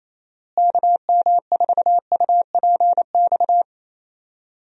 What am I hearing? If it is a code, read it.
Morse code, heard as a keyed tone, about 28 wpm, 700 Hz. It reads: KM4UPX
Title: USA Call Signs - 6 Characters - Repeated 2x - 28wpm